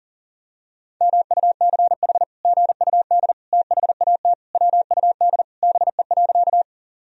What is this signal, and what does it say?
Morse code, heard as a keyed tone, about 37 wpm, 700 Hz. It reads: MUCH GUD THAT WUD BE.
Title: Sentences from Top 200 Words and Abbreviations 40wpm